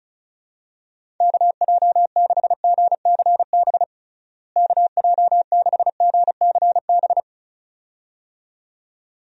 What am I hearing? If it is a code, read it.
Morse code, heard as a keyed tone, about 36 wpm, 700 Hz. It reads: KJ6GCB KJ6GCB
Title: USA Call Signs - 6 Characters - Repeated 2x - 35wpm